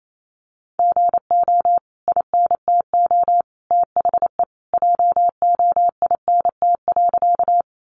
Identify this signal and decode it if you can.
Morse code, heard as a keyed tone, about 28 wpm, 700 Hz. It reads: GO INTO THE JOINT.
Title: Sentences from Top 1300 Words 28wpm